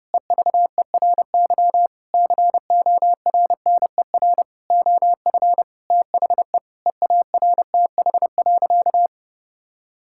Morse code, heard as a keyed tone, about 30 wpm, 700 Hz. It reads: EVERY CORNER OF THE EARTH.
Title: Sentences from Top 1600 Words 30wpm